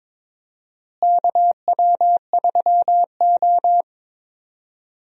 Morse code, heard as a keyed tone, about 22 wpm, 700 Hz. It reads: KW3O
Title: USA Call Signs - 4 Characters - Repeated 3x - 22wpm